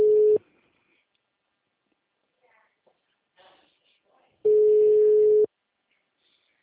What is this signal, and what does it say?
Ringback tone (Europe, 425 Hz).
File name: out- -2016- - - .1231